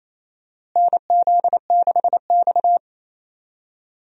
Morse code, heard as a keyed tone, about 28 wpm, 700 Hz. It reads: NZ6X
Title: USA Call Signs - 4 Characters - Repeated 2x - 28wpm